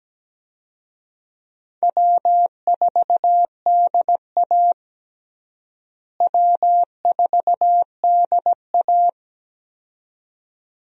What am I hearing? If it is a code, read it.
Morse code, heard as a keyed tone, about 17 wpm, 700 Hz. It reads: W4DA W4DA
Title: USA Call Signs - 4 Characters - Repeated 2x - 17wpm